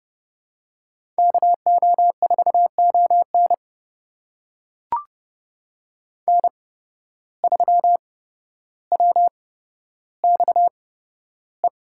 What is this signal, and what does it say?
Morse code, heard as a keyed tone, about 30 wpm, spaced as for 13 wpm, 700 Hz. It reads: KO4ON N3WXE
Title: USA Call Signs - 8x Character Spacing - 30wpm